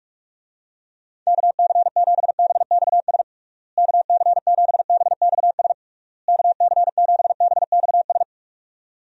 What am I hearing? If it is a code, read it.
Morse code, heard as a keyed tone, about 43 wpm, 700 Hz. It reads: KC7BXS KC7BXS KC7BXS
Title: USA Call Signs - 6 Characters - Repeated 3x - 45wpm